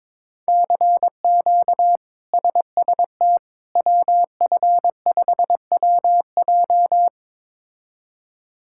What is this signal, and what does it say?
Morse code, heard as a keyed tone, about 22 wpm, 700 Hz. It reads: CQ SST WF5WJ
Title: K1USN SST QSO Practice - 22wpm